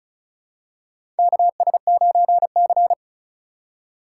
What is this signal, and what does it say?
Morse code, heard as a keyed tone, about 36 wpm, 700 Hz. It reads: KS9C
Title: USA Call Signs - 4 Characters - 3x Character Spacing - 35wpm